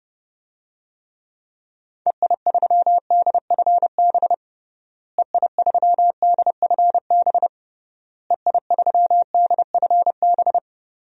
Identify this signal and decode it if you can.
Morse code, heard as a keyed tone, about 30 wpm, 700 Hz. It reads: EI3DFB EI3DFB EI3DFB
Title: DXCC-100 Call Signs - 6 Characters - Repeated 3x - 30wpm